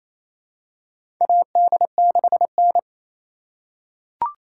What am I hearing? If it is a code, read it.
Morse code, heard as a keyed tone, about 28 wpm, 700 Hz. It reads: AD6N
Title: USA Call Signs - 4 Characters - 3x Character Spacing - 28wpm